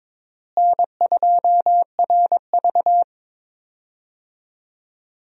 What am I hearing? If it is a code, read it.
Morse code, heard as a keyed tone, about 22 wpm, 700 Hz. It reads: N2RV
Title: USA Call Signs - 4 Characters - Repeated 3x - 22wpm